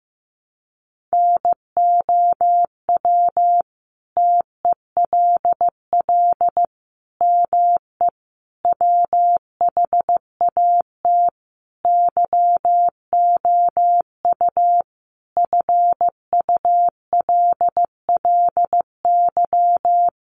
Morse code, heard as a keyed tone, about 15 wpm, 700 Hz. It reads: NOW TELL ME WHAT YOU FULLY
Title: Sentences from Top 1300 Words 15wpm